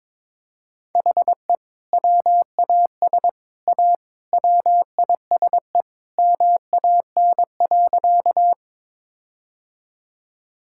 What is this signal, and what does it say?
Morse code, heard as a keyed tone, about 22 wpm, 700 Hz. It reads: HE WAS A WISE MAN.
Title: Sentences from Top 1600 Words 22wpm